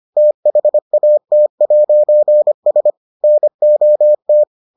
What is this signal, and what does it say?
Morse code, heard as a keyed tone, about 25 wpm, 600 Hz. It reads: THAT'S NOT